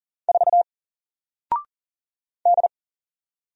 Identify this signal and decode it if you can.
Morse code, heard as a keyed tone, about 39 wpm, 700 Hz. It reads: 4 D
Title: Single Letter-Number Alt 40wpm